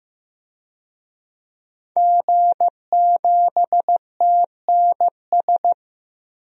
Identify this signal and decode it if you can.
Morse code, heard as a keyed tone, about 15 wpm, 700 Hz. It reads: G7TNS
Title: DXCC-100 Call Signs - 5 Characters - 15wpm